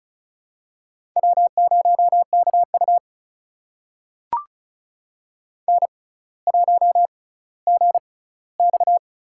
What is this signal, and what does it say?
Morse code, heard as a keyed tone, about 34 wpm, spaced as for 18 wpm, 700 Hz. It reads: W0KU N1GX
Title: USA Call Signs - 6x Character Spacing - 35wpm